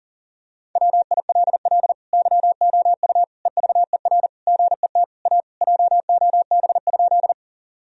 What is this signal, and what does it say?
Morse code, heard as a keyed tone, about 40 wpm, 700 Hz. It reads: WILL YOU EVER GET A JOB?
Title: Sentences from Top 1000 Words - Review Part 3 - 40wpm